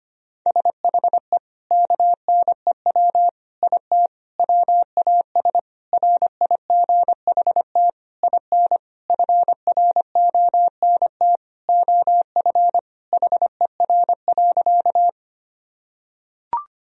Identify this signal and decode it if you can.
Morse code, heard as a keyed tone, about 25 wpm, 700 Hz. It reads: SHE KNEW IT WAS RIGHT IN FRONT OF HER.
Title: Sentences from Top 1000 Words - Review Part 2 - 25wpm